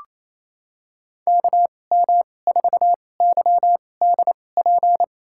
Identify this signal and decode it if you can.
Morse code, heard as a keyed tone, about 28 wpm, spaced as for 20 wpm, 700 Hz. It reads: KM4YDP
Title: USA Call Signs - 2x Character Spacing - 28wpm